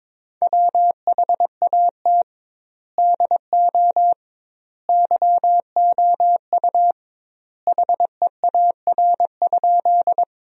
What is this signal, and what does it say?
Morse code, heard as a keyed tone, about 21 wpm, 700 Hz. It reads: WHAT DO YOU HEAR?
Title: 2x Word-Spacing - Sentences from Top 300 Words - 22wpm